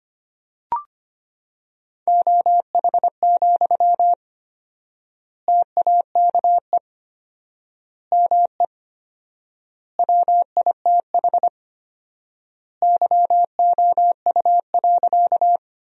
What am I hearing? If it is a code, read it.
Morse code, heard as a keyed tone, about 25 wpm, 700 Hz. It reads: OH, TAKE ME WITH YOU.
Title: 4x Word-Spacing - Sentences from Top 500 Words - 25wpm